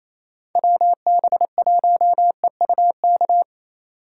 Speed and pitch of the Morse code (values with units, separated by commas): 37 wpm, 700 Hz